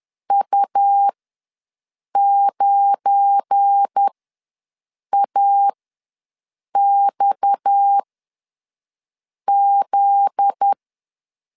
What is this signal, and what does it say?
Morse code, heard as a keyed tone, about 10 wpm, spaced as for 6 wpm, 800 Hz. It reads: U9AXZ
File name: LEZIONI / 9 28 ws 10wpm 4124